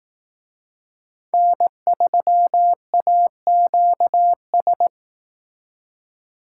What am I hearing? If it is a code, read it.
Morse code, heard as a keyed tone, about 18 wpm, 700 Hz. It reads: N3AQS
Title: Speed Racing - USA 1x3 Call Signs 15wpm